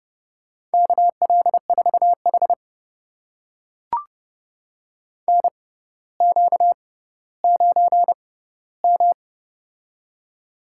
Morse code, heard as a keyed tone, about 30 wpm, spaced as for 15 wpm, 700 Hz. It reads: KL4H NQ9M
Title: USA Call Signs - 4 Characters - 6x Character Spacing - 30wpm